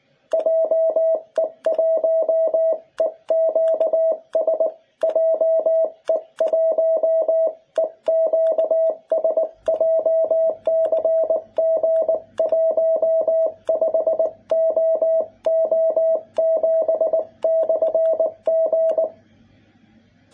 Morse code, heard as a keyed tone, about 20 wpm, 650 Hz. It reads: JE1EQS JE1EQS JCG15OO7/G
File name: jh8cbh / 2021cw